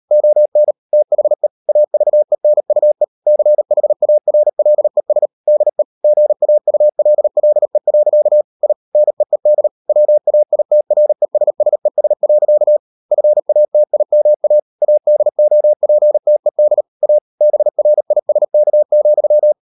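Morse code, heard as a keyed tone, about 38 wpm, 600 Hz. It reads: ON THE AVENUE CHARLES DE GAULLE. I NEED WAITRESSES. FATIMA ADOPTED A BRISK,
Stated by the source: Morse code practice at 38 WPM, sending Ken Follett's novel Never in English, from CW med Gnister.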